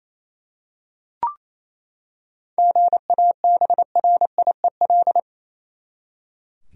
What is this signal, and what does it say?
Morse code, heard as a keyed tone, about 28 wpm, 700 Hz. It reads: GABRIEL
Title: USA Common Names - 28wpm